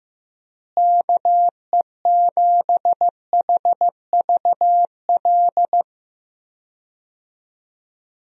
Morse code, heard as a keyed tone, about 15 wpm, 700 Hz. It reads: KE7HVL
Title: USA Call Signs - 6 Characters - Repeated 2x - 15wpm